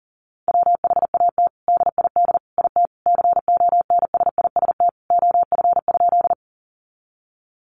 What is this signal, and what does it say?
Morse code, heard as a keyed tone, about 40 wpm, 700 Hz. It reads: WHAT DID IT CONSIST OF?
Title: Sentences from Top 1400 Words 40wpm